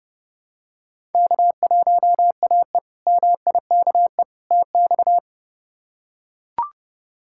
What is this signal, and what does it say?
Morse code, heard as a keyed tone, about 30 wpm, 700 Hz. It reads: K1AE MIKE TX
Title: K1USN SST QSO Practice - 30wpm